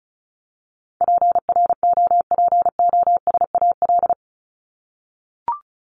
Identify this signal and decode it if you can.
Morse code, heard as a keyed tone, about 34 wpm, 700 Hz. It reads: PROPOSAL
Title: Top 1751-2000 Words 35wpm